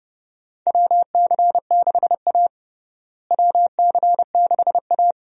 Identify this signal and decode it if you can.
Morse code, heard as a keyed tone, about 32 wpm, 700 Hz. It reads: WC6A WC6A
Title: USA Call Signs - 4 Characters - Repeated 3x - 30wpm